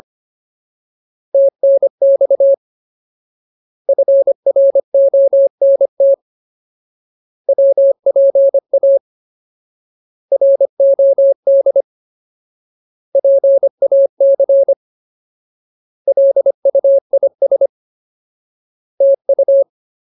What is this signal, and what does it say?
Morse code, heard as a keyed tone, about 25 wpm, 550 Hz. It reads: TNX FRONT WPA ROD PAC LUIS TU
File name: Ultimate Word List / 25WPM Ultimate